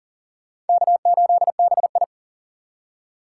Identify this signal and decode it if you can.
Morse code, heard as a keyed tone, about 52 wpm, 700 Hz. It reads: K8BI